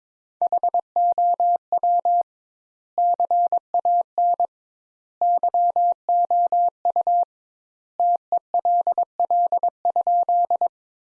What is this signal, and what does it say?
Morse code, heard as a keyed tone, about 21 wpm, 700 Hz. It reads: HOW CAN YOU TELL?
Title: Sentences Top 200 Words - 22wpm